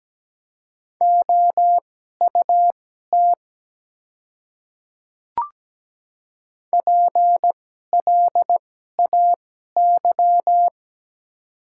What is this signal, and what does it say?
Morse code, heard as a keyed tone, about 17 wpm, spaced as for 12 wpm, 700 Hz. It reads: OUT PLAY
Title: 2x Character Spacing - Top 200 Words - 17wpm